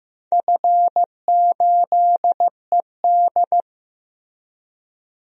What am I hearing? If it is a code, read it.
Morse code, heard as a keyed tone, about 15 wpm, 700 Hz. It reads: F8ED
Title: DXCC-100 Call Signs - 4 Characters - Repeated 3x - 15wpm